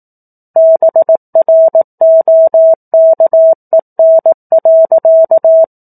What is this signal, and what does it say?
Morse code, heard as a keyed tone, about 18 wpm, 650 Hz. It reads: BROKEN.
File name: 04-Chapter-04 18 WPM